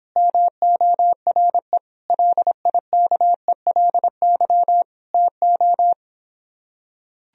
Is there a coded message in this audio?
Yes: Morse code from 0.2 to 5.9 s, heard as a keyed tone, about 26 wpm, 700 Hz. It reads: MORE LIKELY TO